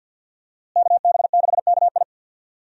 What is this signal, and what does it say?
Morse code, heard as a keyed tone, about 51 wpm, 700 Hz. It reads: KB6KI
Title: USA Call Signs - 5 Characters - Repeated 3x - 50wpm